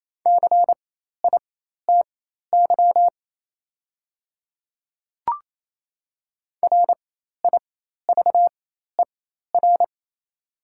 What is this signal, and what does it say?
Morse code, heard as a keyed tone, about 28 wpm, spaced as for 13 wpm, 700 Hz. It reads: CITY RIVER
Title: 4x Character Spacing - Top 500 Words - 28wpm